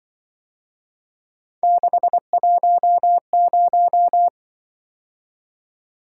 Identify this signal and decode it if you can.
Morse code, heard as a keyed tone, about 25 wpm, 700 Hz. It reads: 610
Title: Speed-Racing - Three Letter-Number Combinations - 20wpm